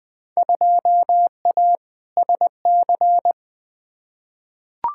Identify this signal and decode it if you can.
Morse code, heard as a keyed tone, about 21 wpm, 700 Hz. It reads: 2A SC
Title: Field Day Exchanges - 20wpm